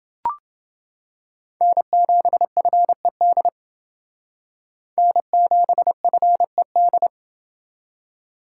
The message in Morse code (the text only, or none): N7FED N7FED